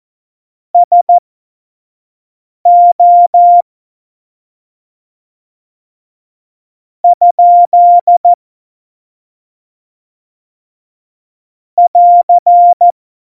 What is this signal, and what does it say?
Morse code, heard as a keyed tone, about 14 wpm, spaced as for 5 wpm, 700 Hz. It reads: SO ? <AR>
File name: Morsekurs / Witze / Witz 31.Juli 2018, 25 F70